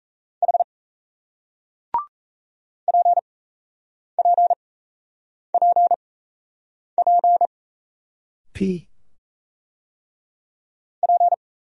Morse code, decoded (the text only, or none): H PPPP P